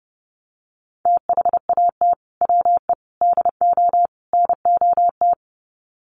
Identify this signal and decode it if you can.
Morse code, heard as a keyed tone, about 30 wpm, 700 Hz. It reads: THAT WE DO NOT
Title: Sets of 4 Words 30wpm